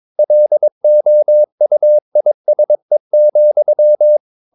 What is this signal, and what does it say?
Morse code, heard as a keyed tone, about 22 wpm, 600 Hz. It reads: LOUISE,
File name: Ferie QRS 22wpm